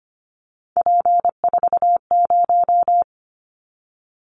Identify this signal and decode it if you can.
Morse code, heard as a keyed tone, about 25 wpm, 700 Hz. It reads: P40
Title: Callsign Prefixes for Top 100 DXCC - 25wpm